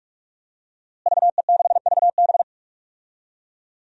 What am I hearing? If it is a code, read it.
Morse code, heard as a keyed tone, about 41 wpm, 700 Hz. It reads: VE6VB
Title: DXCC-100 Call Signs - 5 Characters - Repeated 3x - 45wpm